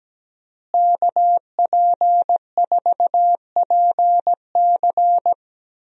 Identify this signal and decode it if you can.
Morse code, heard as a keyed tone, about 17 wpm, 700 Hz. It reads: KP4PC
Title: DXCC-100 Call Signs - 5 Characters - Repeated 3x - 17wpm